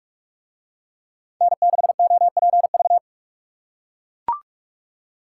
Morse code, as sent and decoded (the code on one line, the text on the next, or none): -. -.... --- .--. ...-
N6OPV